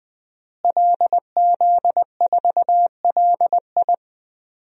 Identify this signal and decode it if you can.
Morse code, heard as a keyed tone, about 20 wpm, 700 Hz. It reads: LZ4LI
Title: DXCC-100 Call Signs - 5 Characters - Repeated 3x - 20wpm